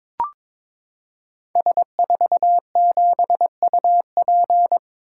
Morse code, heard as a keyed tone, about 22 wpm, 700 Hz. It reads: S47UP
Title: Code Groups - 5 Letter:Numbers - 22wpm